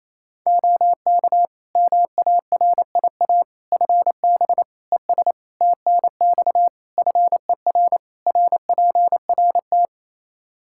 Morse code, heard as a keyed tone, about 28 wpm, 700 Hz. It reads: OK MARIA FB ES TNX FER RPRT
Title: Practice QSO - Exchange 2 - Info - 28wpm